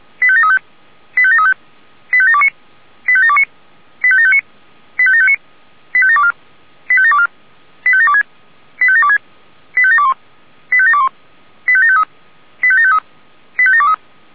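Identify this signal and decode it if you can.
Five-tone selective call (ZVEI), heard as quick runs of five tones, tones of about 70 ms. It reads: 85735 85735 85729 85729 85759 85759 85724 85724 85726 85726 85721 85721 85753 85753 85723
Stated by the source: ZVEI five-tone selective call received on an SDR.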